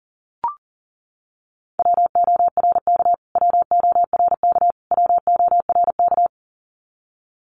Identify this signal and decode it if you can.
Morse code, heard as a keyed tone, about 40 wpm, 700 Hz. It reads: WORK WORK WORK
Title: Top 200 Words Repeated 40wpm